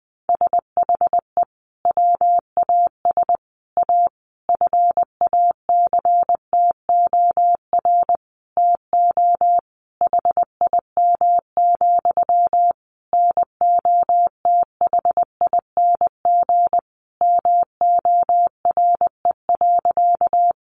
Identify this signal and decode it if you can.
Morse code, heard as a keyed tone, about 20 wpm, 700 Hz. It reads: SHE WAS A FACTOR TO HIM, NOTHING MORE.
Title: Sentences from Top 1300 Words 20wpm